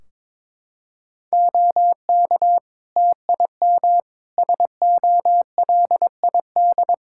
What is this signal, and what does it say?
Morse code, heard as a keyed tone, about 22 wpm, 700 Hz. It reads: OK TIM SOLID
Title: Practice QSO - Exchange 3 - More Info - 22wpm